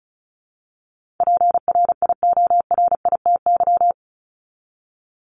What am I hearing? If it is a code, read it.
Morse code, heard as a keyed tone, about 34 wpm, 700 Hz. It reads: PRIORITY
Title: Top 1501-1750 Words 35wpm